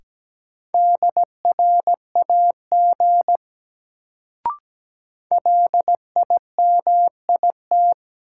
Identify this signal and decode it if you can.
Morse code, heard as a keyed tone, about 17 wpm, 700 Hz. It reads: DRAG LIMIT
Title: Sentences from Top 1500 Words - New Word Drill - 17wpm